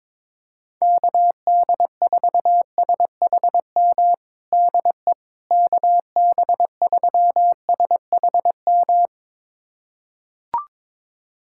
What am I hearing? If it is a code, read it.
Morse code, heard as a keyed tone, about 22 wpm, 700 Hz. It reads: KD4SHM DE KB3SHM
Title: USA Callsign DE Callsign - 6 Characters - 22wpm